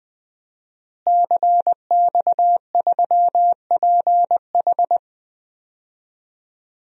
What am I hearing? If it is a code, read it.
Morse code, heard as a keyed tone, about 20 wpm, 700 Hz. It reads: CX3PH
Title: DXCC-100 Call Signs - 5 Characters - Repeated 3x - 20wpm